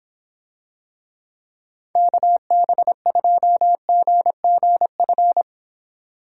Morse code, heard as a keyed tone, about 26 wpm, 700 Hz. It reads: KB2GGF